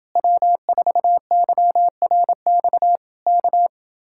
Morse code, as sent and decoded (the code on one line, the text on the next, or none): .-- ....- -.-- .-. -..- / -.-
W4YRX K